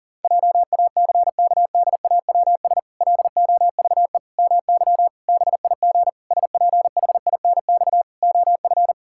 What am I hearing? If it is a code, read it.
Morse code, heard as a keyed tone, about 40 wpm, 700 Hz. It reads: JACKDAWS LOVE MY BIG SPHINX OF
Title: Pangram Challenge v2 - 40wpm